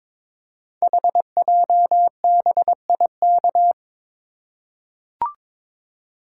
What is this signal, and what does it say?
Morse code, heard as a keyed tone, about 22 wpm, 700 Hz. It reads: HJBIK
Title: Code Groups - 5 Letters - 22wpm